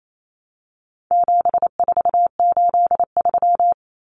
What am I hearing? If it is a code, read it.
Morse code, heard as a keyed tone, about 28 wpm, 700 Hz. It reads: 7483
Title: Common Cut Numbers - 4-Digit 28wpm